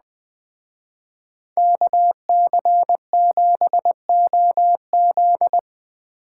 Morse code, heard as a keyed tone, about 21 wpm, 700 Hz. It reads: KC7OZ
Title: Speed Racing - USA 2x2 Call Signs 17wpm